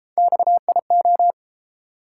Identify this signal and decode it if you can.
Morse code, heard as a keyed tone, about 31 wpm, 700 Hz. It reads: XIO